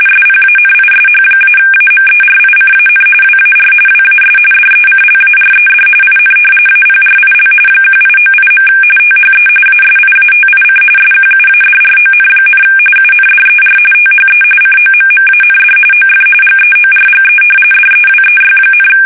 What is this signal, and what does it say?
RTTY, about 45.45 baud. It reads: SPANISH-AMERICAN WAR WAS DECLARED IN APRIL 1898. THE ASIATIC SQUADRON, UNDER THE COMMAND OF GEORGE DEWEY,
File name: fsk850 / usnavy3